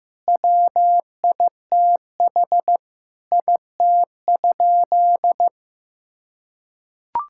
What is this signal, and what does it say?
Morse code, heard as a keyed tone, about 15 wpm, 700 Hz. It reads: WITH IT?
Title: Sentences from Top 1000 Words - Review Part 5 - 15wpm